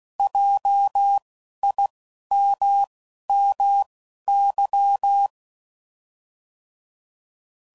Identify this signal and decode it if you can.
Morse code, heard as a keyed tone, about 16 wpm, spaced as for 12 wpm, 800 Hz. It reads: JIMMY